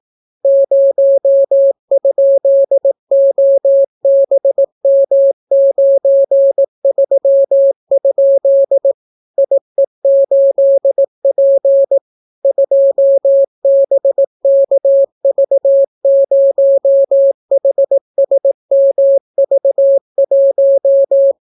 Morse code, heard as a keyed tone, about 18 wpm, 550 Hz. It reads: 0?OBM93? IE8P 2BKV0HSMV1